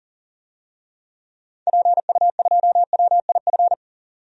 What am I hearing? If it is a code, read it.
Morse code, heard as a keyed tone, about 40 wpm, 700 Hz. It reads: PU2WIF
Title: DXCC-100 Call Signs - 6 Characters - 40wpm